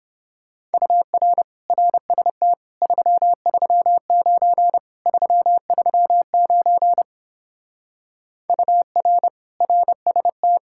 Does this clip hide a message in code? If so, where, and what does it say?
Yes: Morse code from 0.7 to 10.6 s, heard as a keyed tone, about 29 wpm, 700 Hz. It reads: UR RST 339 339 UR RST